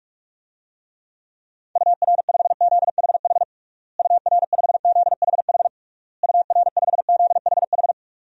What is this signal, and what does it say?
Morse code, heard as a keyed tone, about 46 wpm, 700 Hz. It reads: UR5ZHH UR5ZHH UR5ZHH
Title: DXCC-100 Call Signs - 6 Characters - Repeated 3x - 45wpm